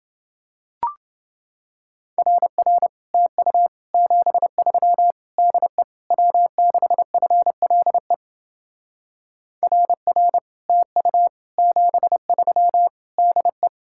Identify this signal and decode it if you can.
Morse code, heard as a keyed tone, about 30 wpm, 700 Hz. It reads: RR TU 73 DE W6FLE RR TU 73 DE